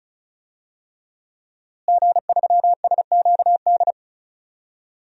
Morse code, heard as a keyed tone, about 36 wpm, 700 Hz. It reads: G3SQD
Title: DXCC-100 Call Signs - 5 Characters - Repeated 3x - 35wpm